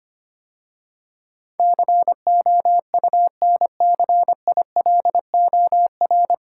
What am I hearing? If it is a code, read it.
Morse code, heard as a keyed tone, about 26 wpm, 700 Hz. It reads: COUNCILOR
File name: practice / 25wpm / 9 Letter Words - First 3 Letters Given - 25wpm